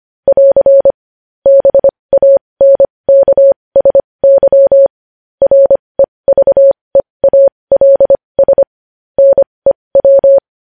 Morse code, heard as a keyed tone, about 25 wpm, spaced as for 20 wpm, 550 Hz. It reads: <AR> BANKSY REVEALS NEW
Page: news-headlines-in-morse-code-at-20-wpm